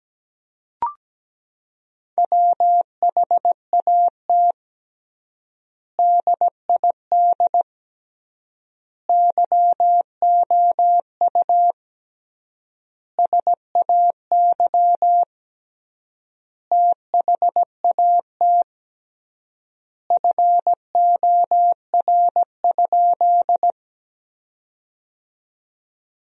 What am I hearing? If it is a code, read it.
Morse code, heard as a keyed tone, about 17 wpm, 700 Hz. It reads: WHAT DID YOU SAY THAT FOR?
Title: 3x Word-Spacing - Sentences from Top 200 Words - 17wpm